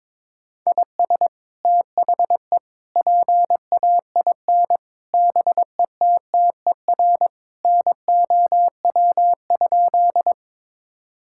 Morse code, heard as a keyed tone, about 22 wpm, 700 Hz. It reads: IS THE PAIN BETTER NOW?
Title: Sentences from Top 1600 Words 22wpm